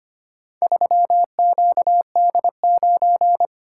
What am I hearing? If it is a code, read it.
Morse code, heard as a keyed tone, about 25 wpm, 700 Hz. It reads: 3QD9
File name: practice / 25wpm / Four Letter-Number Combinations - 25wpm